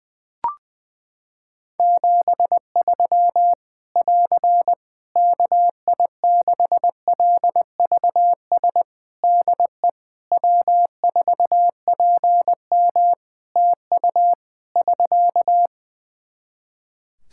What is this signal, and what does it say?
Morse code, heard as a keyed tone, about 20 wpm, 700 Hz. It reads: 73 <AR> KI6LVS DE W4PM TU <SK>
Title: Practice QSO - Exchange 4 - Ending - 20wpm